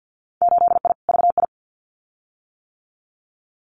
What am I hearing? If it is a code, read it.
Morse code, heard as a keyed tone, about 50 wpm, 700 Hz. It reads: 8I VI
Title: Winter Field Day Exchanges Encore 50wpm